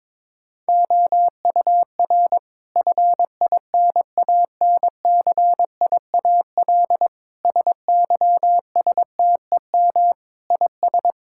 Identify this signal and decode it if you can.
Morse code, heard as a keyed tone, about 22 wpm, 700 Hz. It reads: OUR FINANCIAL SYSTEM IS
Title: Sentences from Top 1000 Words - Review Part 5 - 22wpm